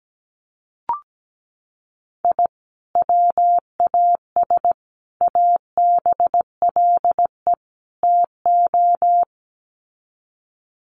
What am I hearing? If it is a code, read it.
Morse code, heard as a keyed tone, about 17 wpm, 700 Hz. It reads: I WAS ABLE TO
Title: Sets of 4 Words 17wpm